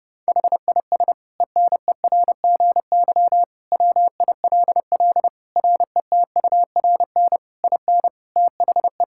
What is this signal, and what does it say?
Morse code, heard as a keyed tone, about 28 wpm, 700 Hz. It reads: HIS ENERGY WILL RETURN IN THE
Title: Sentences from Top 1000 Words - Review Part 6 - 30wpm